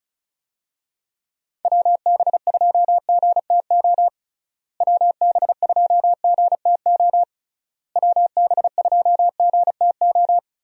Morse code, heard as a keyed tone, about 36 wpm, 700 Hz. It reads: WB2GTO WB2GTO WB2GTO
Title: USA Call Signs - 6 Characters - Repeated 3x - 35wpm